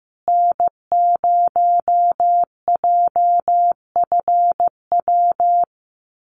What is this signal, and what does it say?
Morse code, heard as a keyed tone, about 15 wpm, 700 Hz. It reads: N0JFW
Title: USA Call Signs Repeated 2x 15wpm